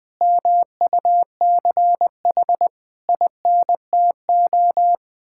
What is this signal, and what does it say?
Morse code, heard as a keyed tone, about 20 wpm, 700 Hz. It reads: MUCH INTO
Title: Sets of 9 Words - 20wpm